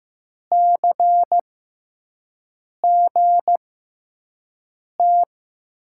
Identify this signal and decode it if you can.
Morse code, heard as a keyed tone, about 15 wpm, spaced as for 5 wpm, 700 Hz. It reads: CGT
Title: USA Call Signs - 5 Characters - 6x Character Spacing - 15wpm